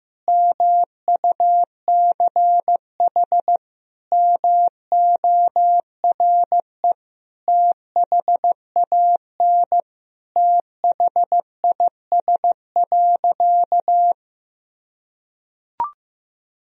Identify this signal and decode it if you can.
Morse code, heard as a keyed tone, about 15 wpm, 700 Hz. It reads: MUCH MORE THAN THIS.
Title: Sentences from Top 1000 Words - Review Part 1 - 15wpm